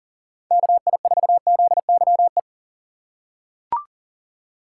Morse code, heard as a keyed tone, about 40 wpm, 700 Hz. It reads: KI4ZYE
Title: USA Call Signs - 6 Characters - 4x Character Spacing - 40wpm